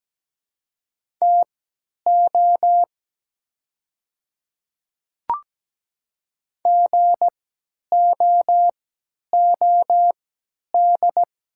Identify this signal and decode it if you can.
Morse code, heard as a keyed tone, about 17 wpm, spaced as for 10 wpm, 700 Hz. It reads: TO GOOD
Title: 3x Character Spacing - Top 200 Words - 17wpm